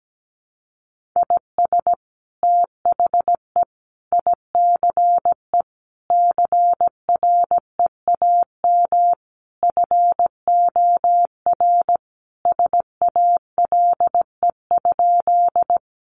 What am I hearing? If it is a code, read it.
Morse code, heard as a keyed tone, about 17 wpm, 700 Hz. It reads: IS THE ICE CREAM FOR SALE?
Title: Sentences from Top 1500 Words 17wpm